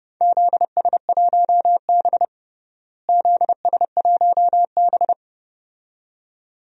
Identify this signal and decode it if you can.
Morse code, heard as a keyed tone, about 30 wpm, 700 Hz. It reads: ZS1B ZS1B
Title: DXCC-100 Call Signs - 4 Characters - Repeated 2x - 30wpm